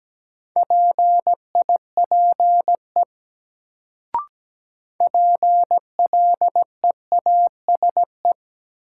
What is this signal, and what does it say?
Morse code, heard as a keyed tone, about 17 wpm, 700 Hz. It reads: PIPE PLEASE
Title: Sentences from Top 1500 Words - New Word Drill - 17wpm